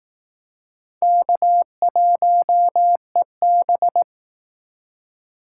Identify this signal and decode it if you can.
Morse code, heard as a keyed tone, about 18 wpm, 700 Hz. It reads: K1EB
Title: Speed Racing - USA 1x2 Call Signs 15wpm